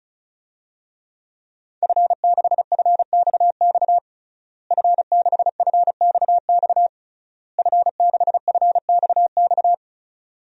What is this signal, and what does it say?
Morse code, heard as a keyed tone, about 36 wpm, 700 Hz. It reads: F6FXX F6FXX F6FXX
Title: DXCC-100 Call Signs - 5 Characters - Repeated 3x - 35wpm